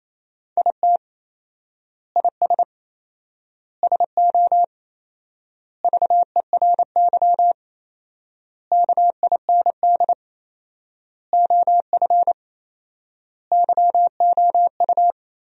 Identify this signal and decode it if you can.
Morse code, heard as a keyed tone, about 28 wpm, 700 Hz. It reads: IT IS SO VERY KIND OF YOU
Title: 4x Word-Spacing - Sentences from Top 300 Words - 28wpm